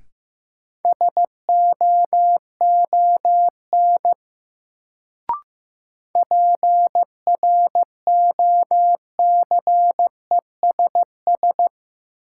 Morse code, heard as a keyed tone, about 15 wpm, 700 Hz. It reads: SOON PROCESS
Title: Sets of 3 Words - New Word Drill - 15wpm